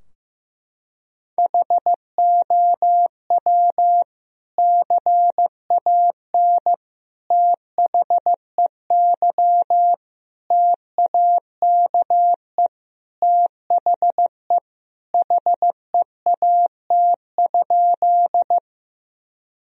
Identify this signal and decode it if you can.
Morse code, heard as a keyed tone, about 15 wpm, 700 Hz. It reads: HOW CAN THEY TAKE THE HEAT?
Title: Sentences from Top 1000 Words - Review Part 3 - 15wpm